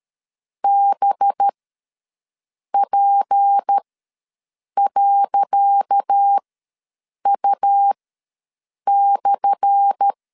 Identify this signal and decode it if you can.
Morse code, heard as a keyed tone, about 13 wpm, spaced as for 6 wpm, 800 Hz. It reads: BP.U/
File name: LEZIONI / 10 SIGNS / 10C DOT ws 12wpm 3548